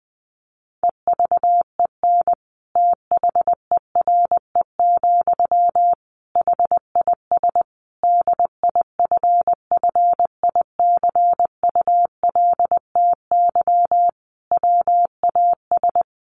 Morse code, heard as a keyed tone, about 19 wpm, 700 Hz. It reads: EVEN THERE, HIS DIFFICULTY WAS
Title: Sentences from Top 1500 Words 20wpm